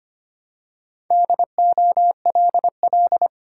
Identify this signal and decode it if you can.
Morse code, heard as a keyed tone, about 25 wpm, 690 Hz. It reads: DOLL
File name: practice / 25wpm / Easy Words with Hints - 25wpm